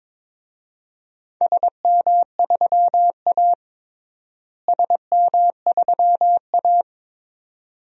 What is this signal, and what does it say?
Morse code, heard as a keyed tone, about 22 wpm, 700 Hz. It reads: SM3A SM3A
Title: DXCC-100 Call Signs - 4 Characters - Repeated 2x - 22wpm